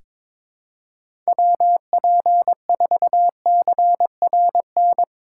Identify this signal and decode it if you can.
Morse code, heard as a keyed tone, about 22 wpm, 700 Hz. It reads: WP4CRN
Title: USA Call Signs - 6 Characters - 2x Character Spacing - 22wpm